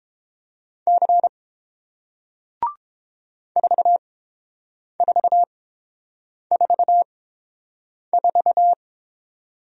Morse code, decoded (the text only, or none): C 4444